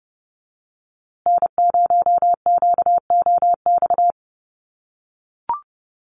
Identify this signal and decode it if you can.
Morse code, heard as a keyed tone, about 30 wpm, 700 Hz. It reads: N0QOX
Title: USA Call Signs Repeated 3x 30wpm